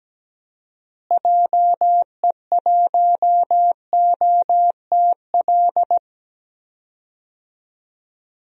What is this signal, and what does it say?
Morse code, heard as a keyed tone, about 17 wpm, 700 Hz. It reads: JE1OTL
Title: DXCC-100 Call Signs - 6 Characters - Repeated 2x - 17wpm